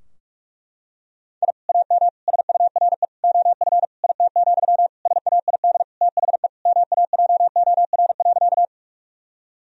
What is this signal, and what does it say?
Morse code, heard as a keyed tone, about 41 wpm, 700 Hz. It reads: I AM SURE OF IT, SAID THE MAJOR.
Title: Sentences from Top 1000 Words - Review Part 4 - 45wpm